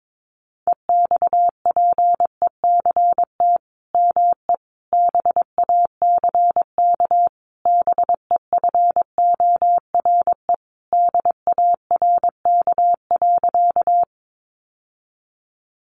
Morse code, heard as a keyed tone, about 22 wpm, 700 Hz. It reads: EXPECT ME BACK BEFORE DARK.
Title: Sentences from Top 1100 Words 22wpm